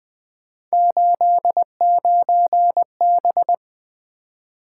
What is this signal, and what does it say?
Morse code, heard as a keyed tone, about 20 wpm, 700 Hz. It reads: 89B
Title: Three Letter-Number Combinations - 20wpm